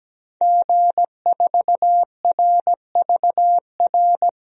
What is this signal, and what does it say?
Morse code, heard as a keyed tone, about 17 wpm, 700 Hz. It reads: G4RVR
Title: DXCC-100 Call Signs - 5 Characters - 17wpm